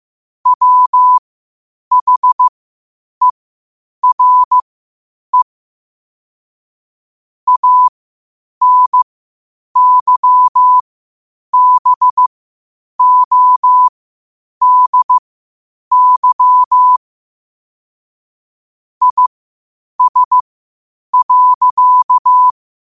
Morse code, heard as a keyed tone, about 15 wpm, spaced as for 8 wpm, 1000 Hz. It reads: WHERE ANYBODY IS.
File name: qotdfiles / 8wpm